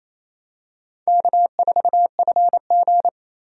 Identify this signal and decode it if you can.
Morse code, heard as a keyed tone, about 28 wpm, 700 Hz. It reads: K4FG
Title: USA Call Signs - 4 Characters - 28wpm